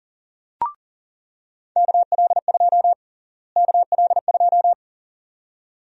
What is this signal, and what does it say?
Morse code, heard as a keyed tone, about 40 wpm, 700 Hz. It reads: KL2 KL2
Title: USA Call Sign Prefixes - Repeated 2x - 40wpm